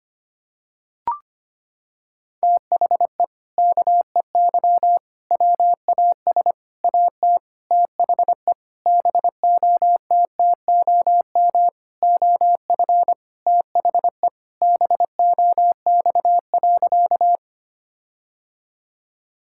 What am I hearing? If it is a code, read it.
Morse code, heard as a keyed tone, about 25 wpm, 700 Hz. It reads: THE KEY WAS AT THE BOTTOM OF THE BOX.
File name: practice / 25wpm / Sentences with 4-Word Phrases - 25wpm